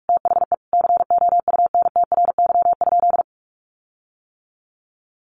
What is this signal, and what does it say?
Morse code, heard as a keyed tone, about 44 wpm, 700 Hz. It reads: THE COUNTRY?
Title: Sentences from Top 1400 Words 45wpm